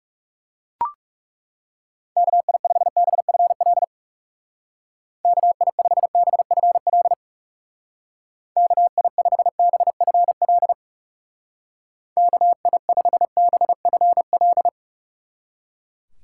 Morse code, heard as a keyed tone, about 37 wpm, 700 Hz. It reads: KI5BFL KI5BFL KI5BFL KI5BFL
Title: Speed Racing - USA Call Signs 30wpm